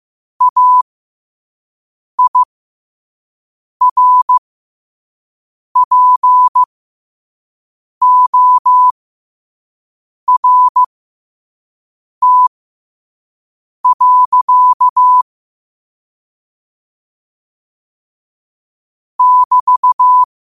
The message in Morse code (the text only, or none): AIRPORT. <BT>